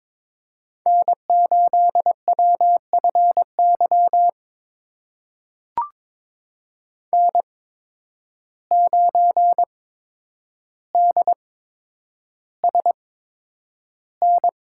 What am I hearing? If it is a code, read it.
Morse code, heard as a keyed tone, about 22 wpm, spaced as for 9 wpm, 700 Hz. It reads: N8WFY N9DSN
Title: USA Call Signs - 5 Characters - 8x Character Spacing - 22wpm